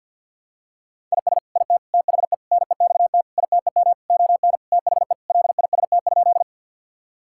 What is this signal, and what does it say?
Morse code, heard as a keyed tone, about 49 wpm, 700 Hz. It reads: IS IT THE NEXT ITEM ON THE LIST?
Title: Sentences from Top 1000 Words - Review Part 6 - 50wpm